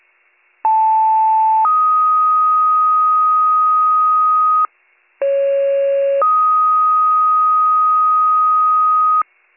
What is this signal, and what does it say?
Two-tone page, heard as a one-second tone, then a three-second tone, about 880 then 1250 Hz, about 550 then 1250 Hz. It is heard from 0.6 to 9.2 s.